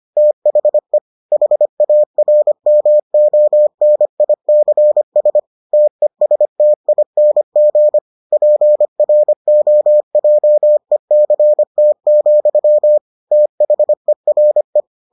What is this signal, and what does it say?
Morse code, heard as a keyed tone, about 25 wpm, 600 Hz. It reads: THE HARMONICS TESTING PROJECT, THERE